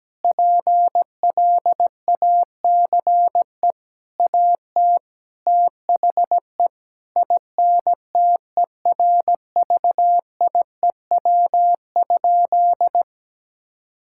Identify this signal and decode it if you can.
Morse code, heard as a keyed tone, about 17 wpm, 700 Hz. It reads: PLACE AT THE INTERVIEW?
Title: Sentences from Top 1600 Words 17wpm